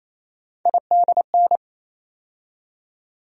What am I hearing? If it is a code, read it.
Morse code, heard as a keyed tone, about 28 wpm, 700 Hz. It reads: IDN
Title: USA Call Sign Postfixes - 3 Letters 28wpm